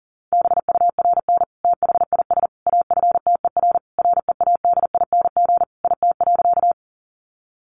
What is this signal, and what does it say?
Morse code, heard as a keyed tone, about 37 wpm, 700 Hz. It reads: BURN THIS AFTER READING IT.
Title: Sentences from Top 1300 Words 40wpm